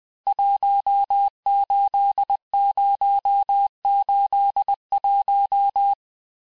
Morse code, heard as a keyed tone, about 20 wpm, 800 Hz. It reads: 18081
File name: cours-du-mercredi / audios / cours-870-FIN-de-QSO-vitesse-19-MPM